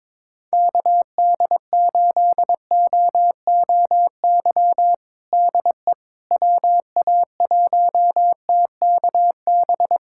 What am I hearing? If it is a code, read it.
Morse code, heard as a keyed tone, about 22 wpm, 700 Hz. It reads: KD8OOY DE WA1TKB
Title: USA Callsign DE Callsign - 6 Characters - 22wpm